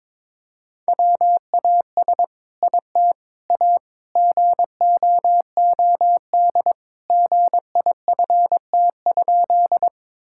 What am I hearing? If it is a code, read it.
Morse code, heard as a keyed tone, about 22 wpm, 700 Hz. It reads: WAS IT A GOOD GIFT?
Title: Sentences from Top 1000 Words - Review Part 6 - 22wpm